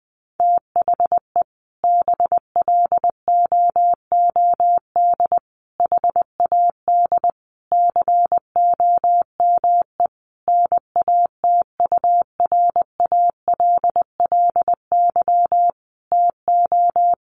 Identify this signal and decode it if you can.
Morse code, heard as a keyed tone, about 20 wpm, 700 Hz. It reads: THE BLOOD HAD COME NATURALLY TO
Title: Sentences from Top 1300 Words 20wpm